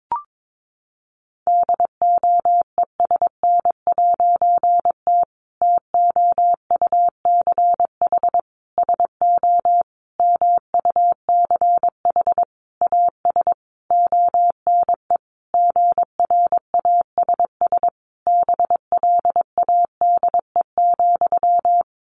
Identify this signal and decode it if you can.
Morse code, heard as a keyed tone, about 22 wpm, 700 Hz. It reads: DOESN'T TOUCH SO MUCH AS ONE GRASS BLADE,
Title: A Sound Of Thunder 22wpm